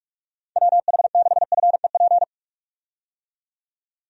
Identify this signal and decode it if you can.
Morse code, heard as a keyed tone, about 47 wpm, 700 Hz. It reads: WH6FEP
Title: USA Call Signs - 6 Characters - 45wpm